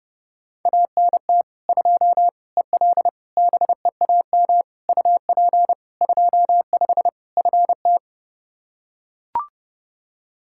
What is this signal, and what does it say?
Morse code, heard as a keyed tone, about 30 wpm, 700 Hz. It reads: ANT 2 EL BEAM UP 25 FT
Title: Practice QSO - Exchange 2 - Info - 30wpm